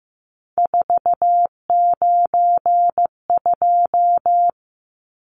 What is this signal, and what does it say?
Morse code, heard as a keyed tone, about 15 wpm, 700 Hz. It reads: 492
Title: Numbers - 3-Digit 15wpm